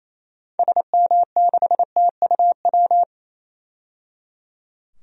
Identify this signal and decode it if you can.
Morse code, heard as a keyed tone, about 28 wpm, 700 Hz. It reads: SM6TUW
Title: DXCC-100 Call Signs - 6 Characters - 28wpm